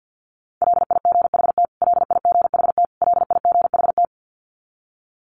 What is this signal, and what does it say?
Morse code, heard as a keyed tone, about 50 wpm, 700 Hz. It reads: LIGHT LIGHT LIGHT
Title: Top 300 Words Repeated 50wpm